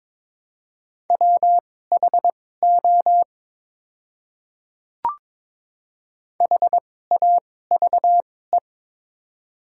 Morse code, heard as a keyed tone, about 22 wpm, spaced as for 16 wpm, 700 Hz. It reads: WHO HAVE
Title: 2x Character Spacing - Top 100 Words - 22wpm